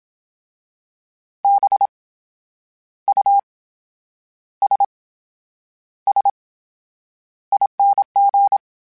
Morse code, heard as a keyed tone, about 27 wpm, spaced as for 9 wpm, 800 Hz. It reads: BUSSING
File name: Special INGs / Special INGs-mp3s / VerySlowToFaster Ing @25